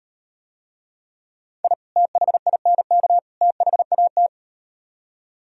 Morse code, heard as a keyed tone, about 38 wpm, 700 Hz. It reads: I THINK THAT